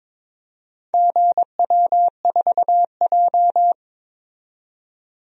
Morse code, heard as a keyed tone, about 22 wpm, 700 Hz. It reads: GW4J
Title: DXCC-100 Call Signs - 4 Characters - 22wpm